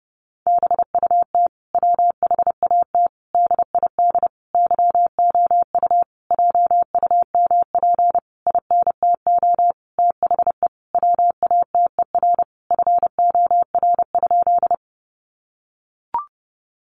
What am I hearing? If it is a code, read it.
Morse code, heard as a keyed tone, about 30 wpm, 700 Hz. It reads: BUT WHAT DID YOU JUMP INTO THE WATER FOR?
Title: Sentences from Top 1300 Words 30wpm